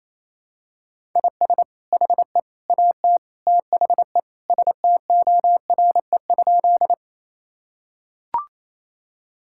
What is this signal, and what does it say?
Morse code, heard as a keyed tone, about 28 wpm, 700 Hz. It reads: IS HE AT THE STORE?
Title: Sentences from Top 1000 Words - Review Part 5 - 28wpm